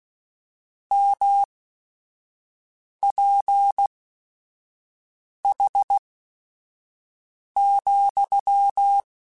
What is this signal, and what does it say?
Morse code, heard as a keyed tone, about 16 wpm, spaced as for 5 wpm, 800 Hz. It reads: MPH,
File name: Weather Sets / Weather Sets-mp3s / Weather 5 22 @5